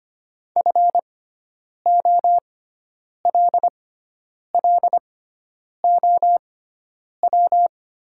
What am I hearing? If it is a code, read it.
Morse code, heard as a keyed tone, about 25 wpm, spaced as for 9 wpm, 700 Hz. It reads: FOLLOW
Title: 6x Character Spacing - Top 500 Words - 25wpm